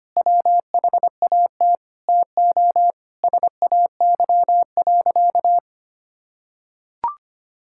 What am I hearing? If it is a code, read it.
Morse code, heard as a keyed tone, about 25 wpm, 700 Hz. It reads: WHAT TO SAY.
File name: practice / 25wpm / Sentences with 4-Word Phrases - 25wpm